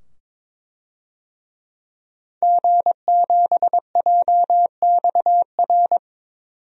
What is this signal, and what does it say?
Morse code, heard as a keyed tone, about 22 wpm, 700 Hz. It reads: G7JXR
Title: DXCC-100 Call Signs - 5 Characters - 22wpm